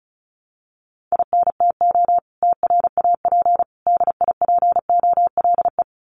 Morse code, heard as a keyed tone, about 34 wpm, 700 Hz. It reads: INTO TRAP DIPOLE
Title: Common QSO Phrases 35wpm